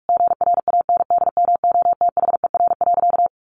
Morse code, heard as a keyed tone, about 44 wpm, 700 Hz. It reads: GRANDMOTHER.
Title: Sentences from Top 1200 Words 45wpm